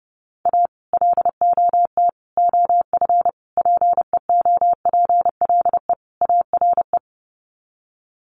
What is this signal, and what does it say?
Morse code, heard as a keyed tone, about 30 wpm, 700 Hz. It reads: A LOT OF PEOPLE ARE
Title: Sets of 5 Words 30wpm